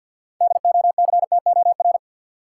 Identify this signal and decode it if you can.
Morse code, heard as a keyed tone, about 51 wpm, 700 Hz. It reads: DOCTOR
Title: Sentences from Top 1500 Words - New Word Drill Alt - 50wpm